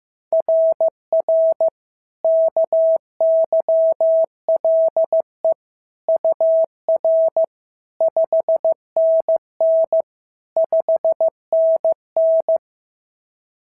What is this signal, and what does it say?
Morse code, heard as a keyed tone, about 14 wpm, 650 Hz. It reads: RR KYLE UR 5NN 5NN
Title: SOTA QSO Practice 15wpm